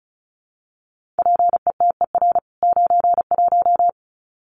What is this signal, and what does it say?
Morse code, heard as a keyed tone, about 34 wpm, 700 Hz. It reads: PETER 91
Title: CWT Exchange 35wpm